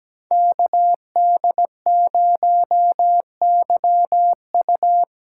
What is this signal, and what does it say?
Morse code, heard as a keyed tone, about 19 wpm, 700 Hz. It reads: KD0YU
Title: Speed Racing - USA 2x2 Call Signs 17wpm